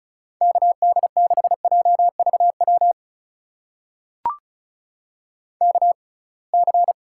Morse code, heard as a keyed tone, about 36 wpm, spaced as for 27 wpm, 700 Hz. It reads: KD6JVW KC
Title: USA Call Signs - 6 Characters - 6x Character Spacing - 35wpm